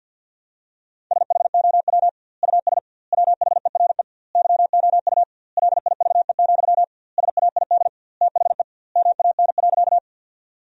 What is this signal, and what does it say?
Morse code, heard as a keyed tone, about 50 wpm, 700 Hz. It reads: SHOW US WHERE YOU LIVE, SAID THE MAN.
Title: 2x Word-Spacing - Sentences from Top 200 Words - 50wpm